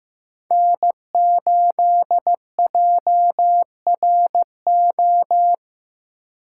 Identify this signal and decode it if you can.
Morse code, heard as a keyed tone, about 18 wpm, 700 Hz. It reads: N8JRO
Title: Speed Racing - USA 1x3 Call Signs 15wpm